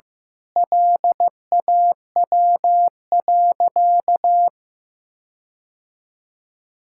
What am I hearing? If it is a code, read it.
Morse code, heard as a keyed tone, about 16 wpm, 700 Hz. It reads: LAW.
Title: Sentences from Top 1600 Words 15wpm